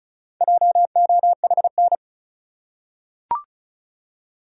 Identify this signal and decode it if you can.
Morse code, heard as a keyed tone, about 34 wpm, 700 Hz. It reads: JOHN
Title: USA Common Names - Max 4 Letters - 35wpm